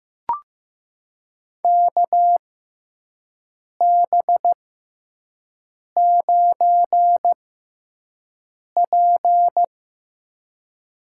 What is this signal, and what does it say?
Morse code, heard as a keyed tone, about 16 wpm, spaced as for 5 wpm, 700 Hz. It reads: KB9P
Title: USA Call Signs - 6 Characters - 6x Character Spacing - 15wpm